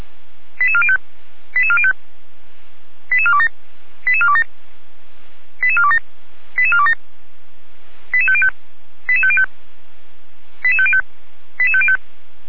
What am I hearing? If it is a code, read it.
Five-tone selective call (ZVEI), heard as quick runs of five tones, tones of about 70 ms. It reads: 88405 88405 88427 88427 88427 88427 88505 88505 88505 88505